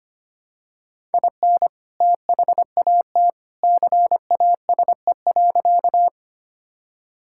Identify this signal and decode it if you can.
Morse code, heard as a keyed tone, about 25 wpm, 700 Hz. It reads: IN THAT CASE.
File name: practice / 25wpm / Sentences with 2-Word Phrases - 25wpm